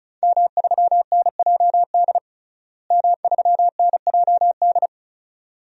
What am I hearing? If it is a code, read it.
Morse code, heard as a keyed tone, about 36 wpm, 700 Hz. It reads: M3NJD M3NJD
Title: DXCC-100 Call Signs - 5 Characters - Repeated 2x - 35wpm